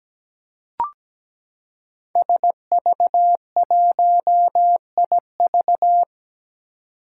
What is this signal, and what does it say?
Morse code, heard as a keyed tone, about 17 wpm, 700 Hz. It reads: SV1IV
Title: Five Challenging Letter-Number Combinations 17wpm